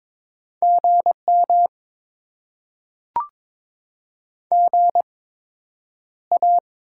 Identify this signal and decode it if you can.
Morse code, heard as a keyed tone, about 22 wpm, spaced as for 9 wpm, 700 Hz. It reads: GM GA
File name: Top 20 QSO Elements - 8x Character Spacing - 22wpm